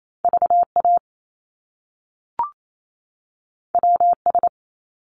Morse code, heard as a keyed tone, about 28 wpm, 700 Hz. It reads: VA WS
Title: Top Two Letter Combinations in Words 28wpm